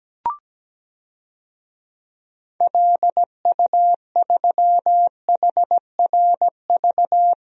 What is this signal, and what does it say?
Morse code, heard as a keyed tone, about 17 wpm, 700 Hz. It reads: LU3HRV
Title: DXCC-100 Call Signs - 6 Characters - Repeated 3x - 17wpm